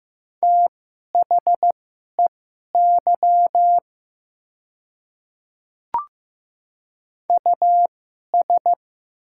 Morse code, heard as a keyed tone, about 15 wpm, spaced as for 11 wpm, 700 Hz. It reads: THEY US
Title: 2x Character Spacing - Top 300 Words - 15wpm